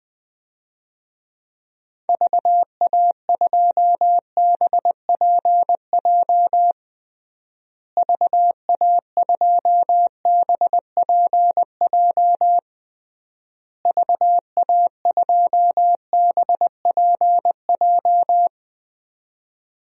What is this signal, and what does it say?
Morse code, heard as a keyed tone, about 21 wpm, 700 Hz. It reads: VA2BPJ VA2BPJ VA2BPJ
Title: DXCC-100 Call Signs - 6 Characters - Repeated 3x - 20wpm